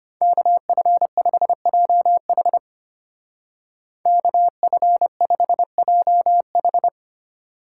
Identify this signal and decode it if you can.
Morse code, heard as a keyed tone, about 31 wpm, 700 Hz. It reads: KF5JH KF5JH
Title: Speed Racing - USA 2x2 Call Signs 25wpm